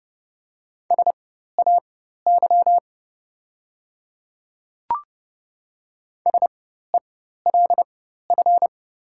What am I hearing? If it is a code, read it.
Morse code, heard as a keyed tone, about 30 wpm, spaced as for 14 wpm, 700 Hz. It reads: SAY SELF
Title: 4x Character Spacing - Top 500 Words - 30wpm